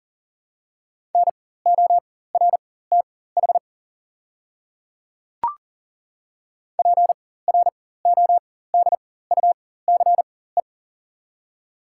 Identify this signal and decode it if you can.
Morse code, heard as a keyed tone, about 40 wpm, spaced as for 19 wpm, 690 Hz. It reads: NORTH PRODUCE
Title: 4x Character Spacing - Top 500 Words - 40wpm